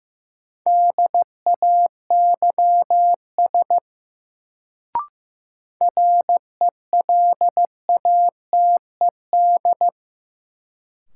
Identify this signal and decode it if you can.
Morse code, heard as a keyed tone, about 15 wpm, 700 Hz. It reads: DAYS RELATED
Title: Sentences from Top 1000 Words - Review Part 5 - New Word Drill - 15wpm